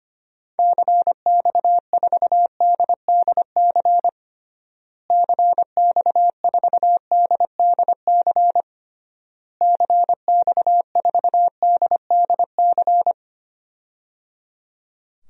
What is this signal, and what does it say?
Morse code, heard as a keyed tone, about 25 wpm, 700 Hz. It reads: CX4DDC CX4DDC CX4DDC
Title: DXCC-100 Call Signs - 6 Characters - Repeated 3x - 25wpm